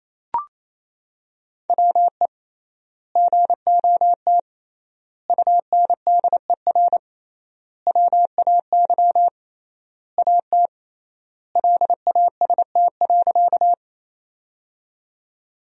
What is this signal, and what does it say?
Morse code, heard as a keyed tone, about 28 wpm, 700 Hz. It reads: WE GOT UNDER WAY AT LAST.
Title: 3x Word-Spacing - Sentences from Top 300 Words - 28wpm